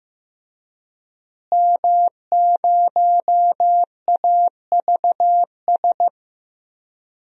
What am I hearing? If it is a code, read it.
Morse code, heard as a keyed tone, about 15 wpm, 700 Hz. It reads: M0AVS
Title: DXCC-100 Call Signs - 5 Characters - Repeated 3x - 15wpm